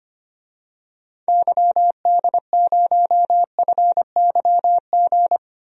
Morse code, heard as a keyed tone, about 25 wpm, 700 Hz. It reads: YD0FYG
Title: DXCC-100 Call Signs - 6 Characters - Repeated 2x - 25wpm